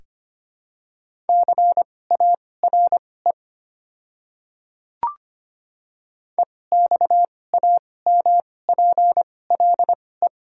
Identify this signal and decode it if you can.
Morse code, heard as a keyed tone, about 25 wpm, spaced as for 18 wpm, 700 Hz. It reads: CARE EXAMPLE
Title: 2x Character Spacing - Top 300 Words - 25wpm